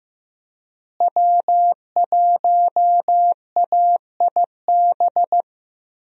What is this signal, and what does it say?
Morse code, heard as a keyed tone, about 15 wpm, 700 Hz. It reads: W1AIB
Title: USA Call Signs - 5 Characters - Repeated 2x - 15wpm